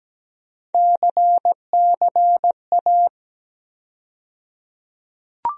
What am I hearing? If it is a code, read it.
Morse code, heard as a keyed tone, about 17 wpm, 700 Hz. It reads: CCA
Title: USA Call Sign Postfixes - 3 Letters 17wpm